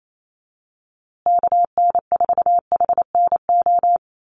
Morse code, heard as a keyed tone, about 28 wpm, 700 Hz. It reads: KN4HNO
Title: USA Call Signs Repeated 3x 28wpm